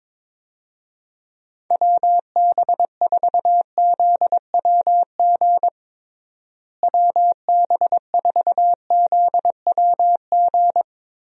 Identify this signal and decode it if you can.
Morse code, heard as a keyed tone, about 22 wpm, 700 Hz. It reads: WB4ZWG WB4ZWG
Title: USA Call Signs - 6 Characters - Repeated 3x - 22wpm